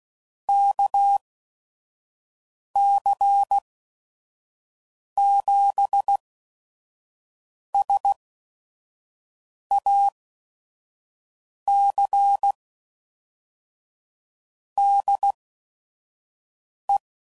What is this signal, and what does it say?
Morse code, heard as a keyed tone, about 16 wpm, spaced as for 5 wpm, 800 Hz. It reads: KC7SAC DE
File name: CW Code Groups @5WPM / QSOs / kc7sac